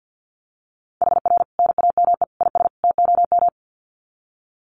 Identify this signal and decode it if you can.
Morse code, heard as a keyed tone, about 50 wpm, 700 Hz. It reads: HR NAME IS TOM
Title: Common QSO Phrases 50wpm